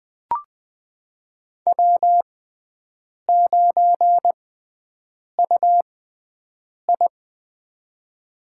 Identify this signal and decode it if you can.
Morse code, heard as a keyed tone, about 20 wpm, spaced as for 7 wpm, 700 Hz. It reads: W9UI
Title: USA Call Signs - 4 Characters - 6x Character Spacing - 20wpm